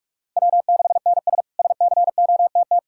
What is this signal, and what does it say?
Morse code, heard as a keyed tone, about 46 wpm, 700 Hz. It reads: W6NS SCOTT
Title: K1USN SST QSO Practice - 45wpm